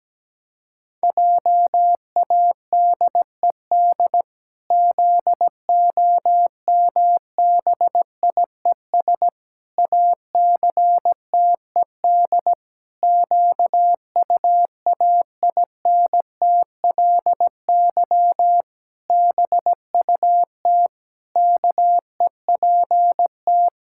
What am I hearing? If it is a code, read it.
Morse code, heard as a keyed tone, about 17 wpm, 700 Hz. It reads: JADED ZOMBIES ACTED QUAINTLY BUT KEPT
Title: Pangram Challenge v2 - 17wpm